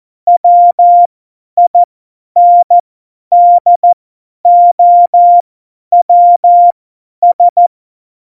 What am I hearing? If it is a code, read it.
Morse code, heard as a keyed tone, about 14 wpm, spaced as for 10 wpm, 700 Hz. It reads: WINDOWS
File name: Morsekurs / Witze / Witz 15. Januar 2019, 50 F70